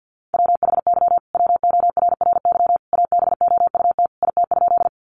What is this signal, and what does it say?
Morse code, heard as a keyed tone, about 50 wpm, 700 Hz. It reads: WHY WORRY ABOUT IT?
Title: Sentences from Top 1100 Words 50wpm